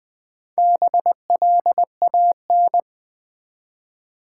Morse code, heard as a keyed tone, about 20 wpm, 700 Hz. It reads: BLAN
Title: Build A Word - 1 Letter At A Time - 3 to 5-Letter Words - 20wpm